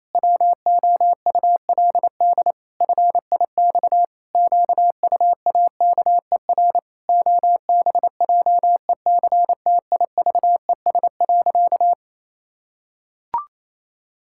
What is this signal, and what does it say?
Morse code, heard as a keyed tone, about 28 wpm, 700 Hz. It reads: WOULD FIX QUAKER OBJECTIVES.
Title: Pangram Challenge v2 - 28wpm